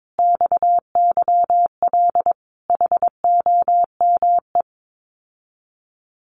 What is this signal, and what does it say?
Morse code, heard as a keyed tone, about 22 wpm, 700 Hz. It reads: XYL HOME
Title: Common QSO Phrases Encore 22wpm